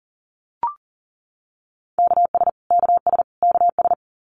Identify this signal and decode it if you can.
Morse code, heard as a keyed tone, about 40 wpm, 700 Hz. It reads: KS KS KS
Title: US States Repeated 3x 40wpm